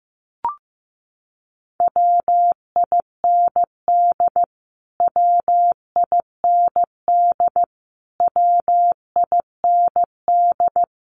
Morse code, heard as a keyed tone, about 15 wpm, 700 Hz. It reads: WIND WIND WIND
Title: Top 500 Words Repeated 15wpm